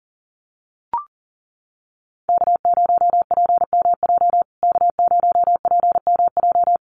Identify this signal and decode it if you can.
Morse code, heard as a keyed tone, about 40 wpm, 700 Hz. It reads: K0PMJ K0PMJ
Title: USA Call Signs Repeated 3x 40wpm